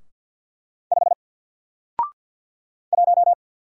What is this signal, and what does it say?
Morse code, heard as a keyed tone, about 48 wpm, 700 Hz. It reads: 5 1
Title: Single Letter-Number Alt 50wpm